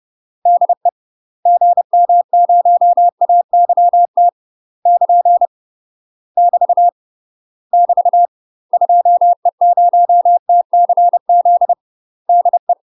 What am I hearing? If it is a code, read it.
Morse code, heard as a keyed tone, about 30 wpm, 700 Hz. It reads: DE GM0AYT <KN> <BT> <BT> 2E0TCZ DE
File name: UK MASTER QSOs TRAINING / 30WPM / UK MASTER QSOs 0044